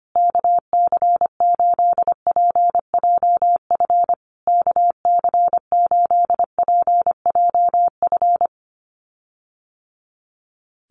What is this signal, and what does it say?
Morse code, heard as a keyed tone, about 25 wpm, 700 Hz. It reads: KC8PJF KC8PJF
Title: USA Call Signs Repeated 3x 25wpm